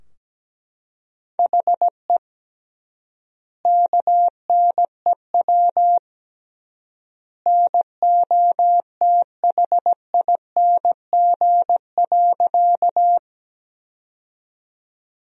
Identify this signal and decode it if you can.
Morse code, heard as a keyed tone, about 17 wpm, 700 Hz. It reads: HE KNEW NOTHING.
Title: 3x Word-Spacing - Sentences from Top 500 Words - 17wpm